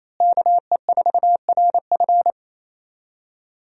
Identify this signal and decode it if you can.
Morse code, heard as a keyed tone, about 28 wpm, 700 Hz. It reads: KE4RF
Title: USA Call Signs - 5 Characters - Repeated 3x - 28wpm